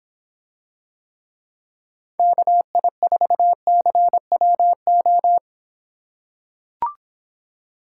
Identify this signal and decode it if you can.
Morse code, heard as a keyed tone, about 26 wpm, 700 Hz. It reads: KI4CWO